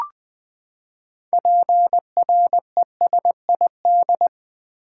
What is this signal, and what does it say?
Morse code, heard as a keyed tone, about 20 wpm, 700 Hz. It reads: PRESID
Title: Build A Word - 6 Letters At A Time - 12-Letter Words - 20wpm